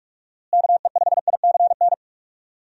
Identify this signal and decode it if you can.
Morse code, heard as a keyed tone, about 46 wpm, 700 Hz. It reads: KE5ICN
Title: USA Call Signs - 6 Characters - Repeated 2x - 45wpm